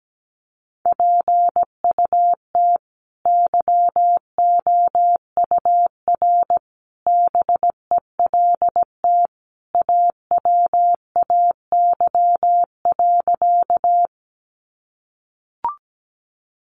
Morse code, heard as a keyed tone, about 17 wpm, 700 Hz. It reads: PUT YOUR BELT AWAY.
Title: Sentences from Top 1400 Words 17wpm